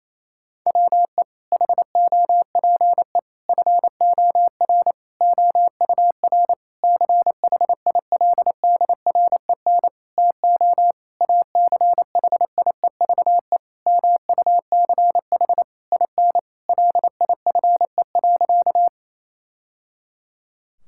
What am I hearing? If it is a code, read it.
Morse code, heard as a keyed tone, about 28 wpm, 700 Hz. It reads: WE HOPE FOR OUR CHILDREN TO ACHIEVE MUCH IN LIFE.
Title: Sentences from Top 1000 Words - Review Part 6 - 28wpm